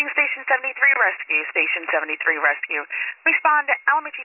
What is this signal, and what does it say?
Emergency Alert System (SAME) tones, heard as harsh buzzing bursts, with speech. It is heard from 0.7 to 4.0 s.